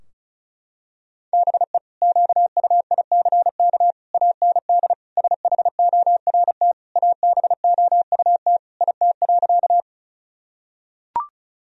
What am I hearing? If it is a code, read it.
Morse code, heard as a keyed tone, about 36 wpm, 700 Hz. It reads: BE QUICK AND SHORT ABOUT IT.
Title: Sentences from Top 1000 Words - Review Part 2 - 35wpm